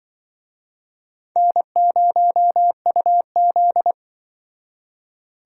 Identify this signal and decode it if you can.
Morse code, heard as a keyed tone, about 25 wpm, 700 Hz. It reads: N0UZ
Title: Speed Racing - USA Call Signs 20wpm